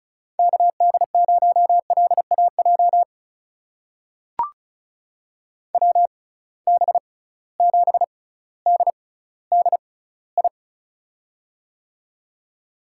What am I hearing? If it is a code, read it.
Morse code, heard as a keyed tone, about 34 wpm, spaced as for 18 wpm, 700 Hz. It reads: KD0LAJ WB7DDI
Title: USA Call Signs - 6 Characters - 6x Character Spacing - 35wpm